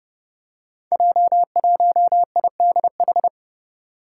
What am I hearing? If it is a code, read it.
Morse code, heard as a keyed tone, about 31 wpm, 700 Hz. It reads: J1IDH
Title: Five Challenging Letter-Number Combinations 30wpm